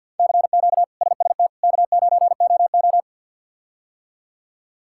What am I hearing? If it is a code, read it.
Morse code, heard as a keyed tone, about 50 wpm, 700 Hz. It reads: CQ SST K9OO
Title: K1USN SST QSO Practice - 50wpm